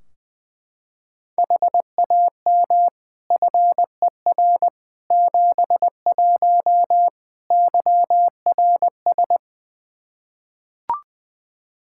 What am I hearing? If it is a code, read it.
Morse code, heard as a keyed tone, about 20 wpm, 700 Hz. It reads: HAM FER 71 YRS
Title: Practice QSO - Exchange 3 - More Info - 20wpm